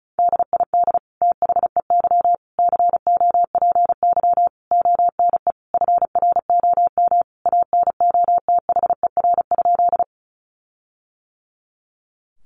Morse code, heard as a keyed tone, about 34 wpm, 700 Hz. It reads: DID THEY COPY ONE FROM ANOTHER?
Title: Sentences from Top 1400 Words 35wpm